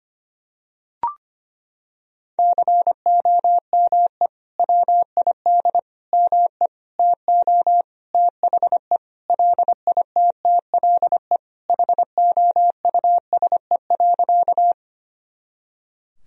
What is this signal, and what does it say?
Morse code, heard as a keyed tone, about 25 wpm, 700 Hz. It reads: COME WID ME TO THE LITTLE HOUSE.
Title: Sentences from Top 200 Words and Abbreviations 25wpm